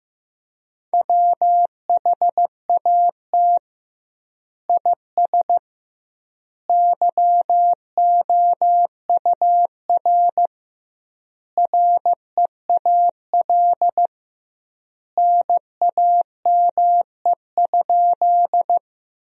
Morse code, heard as a keyed tone, about 15 wpm, 700 Hz. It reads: WHAT IS YOUR REAL NAME?
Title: 2x Word-Spacing - Sentences from Top 300 Words - 15wpm